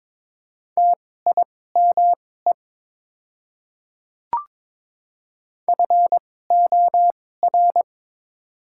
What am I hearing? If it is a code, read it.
Morse code, heard as a keyed tone, about 22 wpm, spaced as for 16 wpm, 700 Hz. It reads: TIME FOR
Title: 2x Character Spacing - Top 100 Words - 22wpm